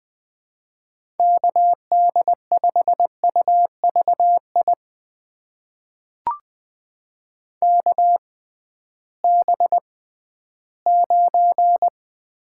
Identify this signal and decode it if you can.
Morse code, heard as a keyed tone, about 20 wpm, spaced as for 13 wpm, 700 Hz. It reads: KD5UVI KB9
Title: USA Call Signs - 6 Characters - 6x Character Spacing - 20wpm